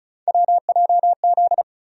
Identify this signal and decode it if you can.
Morse code, heard as a keyed tone, about 36 wpm, 700 Hz. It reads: WJZ
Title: USA Call Sign Postfixes - 3 Letters 35wpm